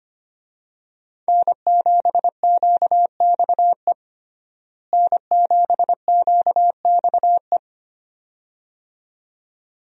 Morse code, heard as a keyed tone, about 25 wpm, 700 Hz. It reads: N7QXE N7QXE
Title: USA Call Signs - 5 Characters - Repeated 2x - 25wpm